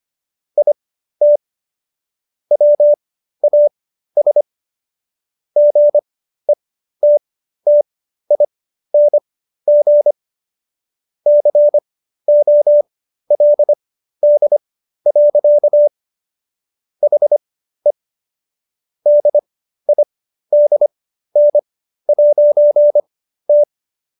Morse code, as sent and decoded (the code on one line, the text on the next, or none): .. - / .-- .- ... / --. - - .. -. --. / -.-. --- .-.. -.. .-.-.- / .... / -.. .. -.. -. .----. -
IT WAS GETTING COLD. HE DIDN'T